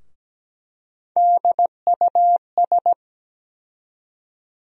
Morse code, heard as a keyed tone, about 17 wpm, 700 Hz. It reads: DUS
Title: USA Call Sign Postfixes - 3 Letters 17wpm